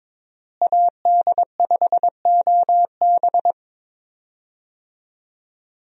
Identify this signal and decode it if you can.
Morse code, heard as a keyed tone, about 22 wpm, 700 Hz. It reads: AD5OB
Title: USA Call Signs - 5 Characters - Repeated 2x - 22wpm